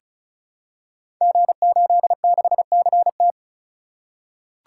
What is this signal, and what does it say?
Morse code, heard as a keyed tone, about 36 wpm, 700 Hz. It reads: G86CT
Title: Code Groups - 5 Letter:Numbers - 35wpm